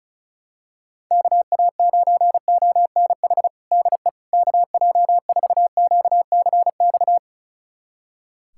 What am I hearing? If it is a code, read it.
Morse code, heard as a keyed tone, about 36 wpm, 700 Hz. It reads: KA9ONH DE KJ4QCX
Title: USA Callsign DE Callsign - 6 Characters - 35wpm